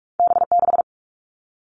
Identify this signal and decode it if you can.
Morse code, heard as a keyed tone, about 46 wpm, 700 Hz. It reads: B6
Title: All Cut Numbers - 2-Digit 45wpm